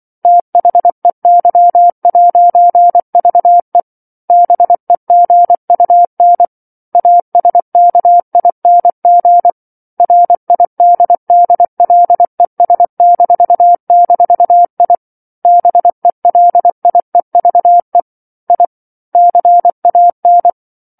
Morse code, heard as a keyed tone, about 24 wpm, 700 Hz. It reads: THEY'VE BEGUN ASKING RIDDLES--I BELIEVE I CAN
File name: 07-Chapter-07 24 WPM